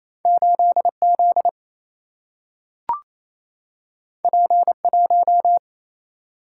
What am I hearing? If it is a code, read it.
Morse code, heard as a keyed tone, about 28 wpm, 700 Hz. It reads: 8Z P1
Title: Single-Letter Single-Number 28wpm